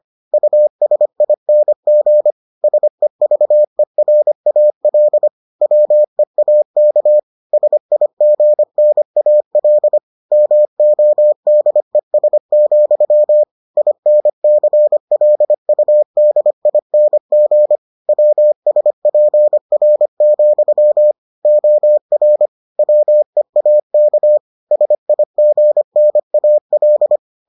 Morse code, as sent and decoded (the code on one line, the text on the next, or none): ..- ... .. -. --. / ... ...- . .-. .- .-.. / .-- . .- -.- / ... .. --. -. .- .-.. / -- --- -.. ... --..-- / .. -. -.-. .-.. ..- -.. .. -. --. / .-- ... .--. .-. --..-- / --- .-. / .-- . .- -.- / ... .. --. -. .- .-..
USING SEVERAL WEAK SIGNAL MODES, INCLUDING WSPR, OR WEAK SIGNAL